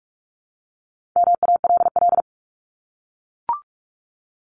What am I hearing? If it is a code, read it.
Morse code, heard as a keyed tone, about 44 wpm, 700 Hz. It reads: MALL
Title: Top 1501-1750 Words 45wpm